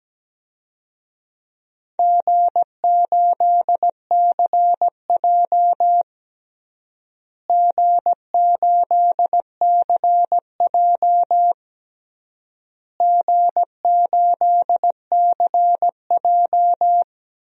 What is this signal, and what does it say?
Morse code, heard as a keyed tone, about 17 wpm, 700 Hz. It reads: G8CJ G8CJ G8CJ
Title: DXCC-100 Call Signs - 4 Characters - Repeated 3x - 17wpm